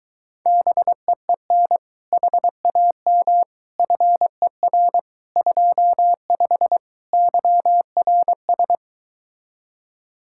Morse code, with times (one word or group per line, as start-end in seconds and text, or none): 0.5-1.8 BEEN
2.1-3.4 HAM
3.8-5.0 FER
5.4-6.8 25
7.1-8.8 YRS